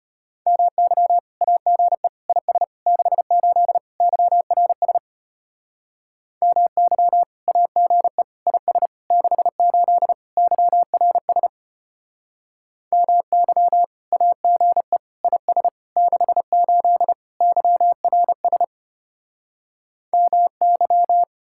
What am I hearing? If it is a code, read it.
Morse code, heard as a keyed tone, about 31 wpm, 700 Hz. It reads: MY AGE IS 68 YRS MY AGE IS 68 YRS MY AGE IS 68 YRS MY
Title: Speed-Racing Common QSO Phrases Encore - 25wpm